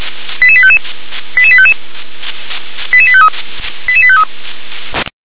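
Five-tone selective call (ZVEI), heard as quick runs of five tones, tones of about 70 ms. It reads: 88855 88855 88853 88853
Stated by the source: ZVEI five-tone selective call 88783 received via SDR.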